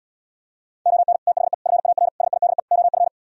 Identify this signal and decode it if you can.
Morse code, heard as a keyed tone, about 77 wpm, 700 Hz. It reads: ON THE FAR SIDE OF